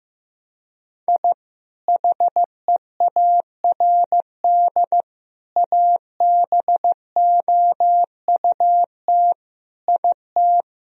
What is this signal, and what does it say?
Morse code, heard as a keyed tone, about 15 wpm, 700 Hz. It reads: I HEARD ABOUT IT
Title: Sentences from Top 1000 Words - Review Part 1 - 15wpm